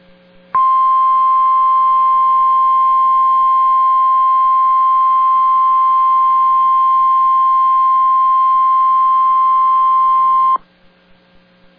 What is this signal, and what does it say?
Emergency Alert System (SAME) tones (the 1050 Hz attention tone). It is heard from 0.5 to 10.6 s.